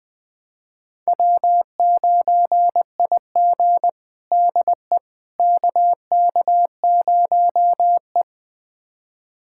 Morse code, heard as a keyed tone, about 20 wpm, 700 Hz. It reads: W9IG DE KK0E
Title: USA Callsign DE Callsign - 4 Characters - 20wpm